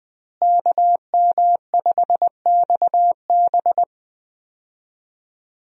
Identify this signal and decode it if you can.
Morse code, heard as a keyed tone, about 21 wpm, 700 Hz. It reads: KM5XB
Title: USA Call Signs - 5 Characters - Repeated 3x - 20wpm